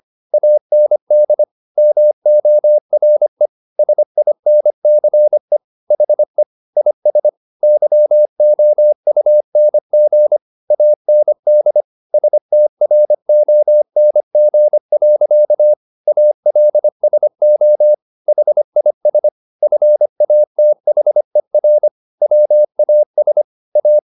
Morse code, heard as a keyed tone, about 25 wpm, 600 Hz. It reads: AND MORE SINCE HE IS YOUNG AND STRONG. ALSO HIS FATHER WAS A
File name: the old man and the sea part 1 25wpm